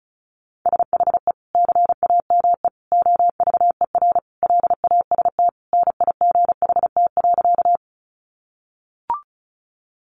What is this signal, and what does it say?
Morse code, heard as a keyed tone, about 34 wpm, 700 Hz. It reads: SHE CAME OVER LAST NIGHT.
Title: Sentences from Top 300 Words 35wpm